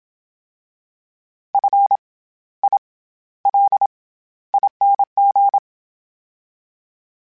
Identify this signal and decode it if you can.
Morse code, heard as a keyed tone, about 27 wpm, 800 Hz. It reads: F I L ING
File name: Special INGs-mp3s / Special INGs-mp3s / SlowToFaster ing @25